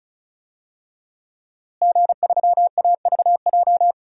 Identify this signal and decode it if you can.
Morse code, heard as a keyed tone, about 34 wpm, 700 Hz. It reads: G3AVJ
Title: DXCC-100 Call Signs - 5 Characters - Repeated 2x - 35wpm